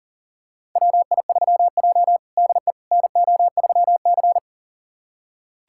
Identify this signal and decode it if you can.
Morse code, heard as a keyed tone, about 40 wpm, 690 Hz. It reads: WI3J DE NO3C
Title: USA Callsign DE Callsign - 4 Characters - 40wpm